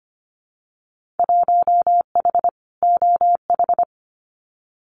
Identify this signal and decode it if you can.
Morse code, heard as a keyed tone, about 25 wpm, 700 Hz. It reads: 1H OH
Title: Winter Field Day Exchanges 25wpm